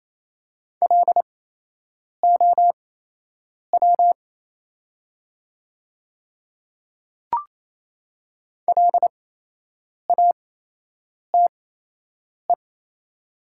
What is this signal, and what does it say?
Morse code, heard as a keyed tone, about 28 wpm, spaced as for 8 wpm, 700 Hz. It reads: LOW LATE
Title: 8x Character Spacing - Top 500 Words - 28wpm